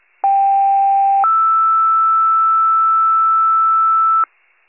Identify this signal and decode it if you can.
Two-tone page, heard as a one-second tone, then a three-second tone, about 790 then 1320 Hz. It is heard from 0.2 to 4.2 s.
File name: pager / 33 WCFM / wcalert